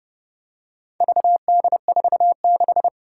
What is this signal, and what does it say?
Morse code, heard as a keyed tone, about 30 wpm, 690 Hz. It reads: VD46
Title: Four Challenging Letter-Number Combinations 30wpm